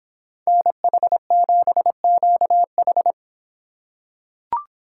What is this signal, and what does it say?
Morse code, heard as a keyed tone, about 24 wpm, 700 Hz. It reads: NH7QH